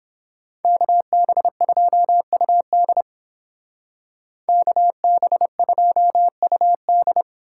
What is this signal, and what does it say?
Morse code, heard as a keyed tone, about 30 wpm, 700 Hz. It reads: KB2UD KB2UD